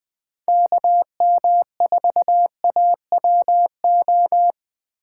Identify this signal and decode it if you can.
Morse code, heard as a keyed tone, about 20 wpm, 700 Hz. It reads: KM4AWO